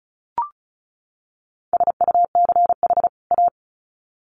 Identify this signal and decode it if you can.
Morse code, heard as a keyed tone, about 34 wpm, 700 Hz. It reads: SUCH A
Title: Sets of 2 Words 35wpm